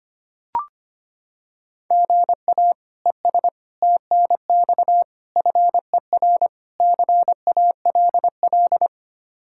Morse code, heard as a keyed tone, about 25 wpm, 700 Hz. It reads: GA ES TNX FER CALL
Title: K1USN SST QSO Practice 25wpm